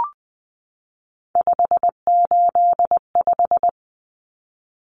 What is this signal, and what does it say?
Morse code, heard as a keyed tone, about 20 wpm, 700 Hz. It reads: 585
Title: Common Cut Numbers - 3-Digit 20wpm